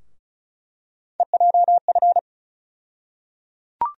Morse code, heard as a keyed tone, about 34 wpm, 700 Hz. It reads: EJF
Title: USA Call Sign Postfixes - 3 Letters 35wpm